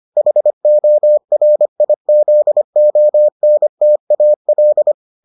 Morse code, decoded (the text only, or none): HORIZONTAL